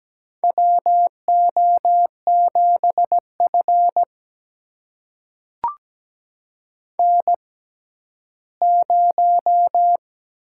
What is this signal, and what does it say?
Morse code, heard as a keyed tone, about 17 wpm, spaced as for 11 wpm, 700 Hz. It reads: WO7F N0
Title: USA Call Signs - 6x Character Spacing - 17wpm